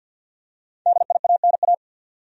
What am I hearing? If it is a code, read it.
Morse code, heard as a keyed tone, about 53 wpm, 700 Hz. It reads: DIANA
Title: USA Common Names - 50wpm